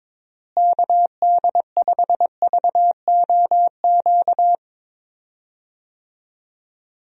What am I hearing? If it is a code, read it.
Morse code, heard as a keyed tone, about 22 wpm, 700 Hz. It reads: KD5VOQ
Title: USA Call Signs - 6 Characters - Repeated 2x - 22wpm